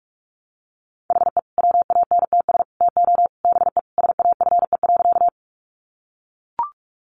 Morse code, heard as a keyed tone, about 44 wpm, 700 Hz. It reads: HE WANTS TO BE SAFE.
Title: Additional Easy Phrases 45wpm